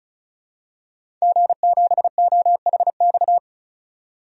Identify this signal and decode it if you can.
Morse code, heard as a keyed tone, about 36 wpm, 700 Hz. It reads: G7OHX
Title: DXCC-100 Call Signs - 5 Characters - 35wpm